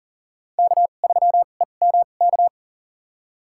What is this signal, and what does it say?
Morse code, heard as a keyed tone, about 40 wpm, spaced as for 29 wpm, 700 Hz. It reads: K3EMK
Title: USA Call Signs - 5 Characters - 2x Character Spacing - 40wpm